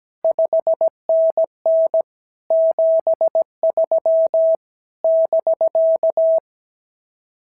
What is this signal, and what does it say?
Morse code, heard as a keyed tone, about 16 wpm, 650 Hz. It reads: 5NN 73 <BK>
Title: SOTA QSO Practice 17wpm